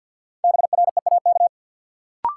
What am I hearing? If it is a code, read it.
Morse code, heard as a keyed tone, about 53 wpm, 700 Hz. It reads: BREAK
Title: Sentences from Top 900 Words - New Word Drill - 50wpm